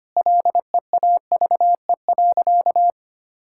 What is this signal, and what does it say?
Morse code, heard as a keyed tone, about 25 wpm, 700 Hz. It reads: LEAVE.
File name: practice / 25wpm / Sentences with 4-Word Phrases - 25wpm